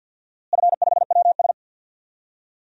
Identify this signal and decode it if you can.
Morse code, heard as a keyed tone, about 53 wpm, 700 Hz. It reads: F5WS
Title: DXCC-100 Call Signs - 4 Characters - Repeated 3x - 50wpm